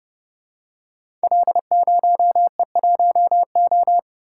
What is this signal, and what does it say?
Morse code, heard as a keyed tone, about 29 wpm, 700 Hz. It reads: L0E1O
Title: Code Groups - 5 Letter:Numbers - 30wpm